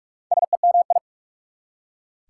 Morse code, heard as a keyed tone, about 44 wpm, 690 Hz. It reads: SEMI
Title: Prefixes and Suffixes 45wpm